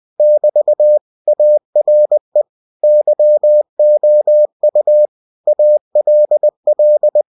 Morse code, heard as a keyed tone, about 20 wpm, 600 Hz. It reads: <BT> ARE YOU ALL